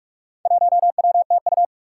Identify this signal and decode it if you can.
Morse code, heard as a keyed tone, about 44 wpm, 700 Hz. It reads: 1WTU
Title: Four Challenging Letter-Number Combinations 45wpm